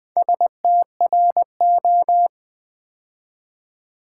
Morse code, heard as a keyed tone, about 20 wpm, 700 Hz. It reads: STRO
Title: Build A Word - 4 Letters At A Time - 8-Letter Words - 20wpm